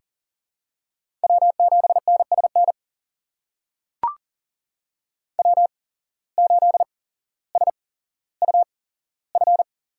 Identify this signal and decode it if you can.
Morse code, heard as a keyed tone, about 40 wpm, spaced as for 17 wpm, 700 Hz. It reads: W7NSN W8SUF
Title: USA Call Signs - 5 Characters - 8x Character Spacing - 40wpm